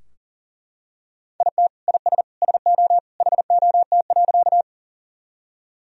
Morse code, heard as a keyed tone, about 40 wpm, 700 Hz. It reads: IT IS SO HOT.
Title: Sentences from Top 1000 Words - Review Part 3 - 40wpm